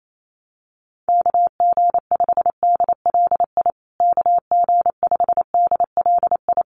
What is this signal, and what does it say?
Morse code, heard as a keyed tone, about 28 wpm, 700 Hz. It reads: KG5DLI KG5DLI
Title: USA Call Signs Repeated 3x 28wpm